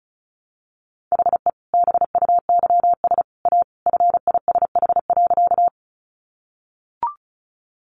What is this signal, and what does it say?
Morse code, heard as a keyed tone, about 34 wpm, 700 Hz. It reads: HE BUYS A FISH.
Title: Additional Easy Phrases 35wpm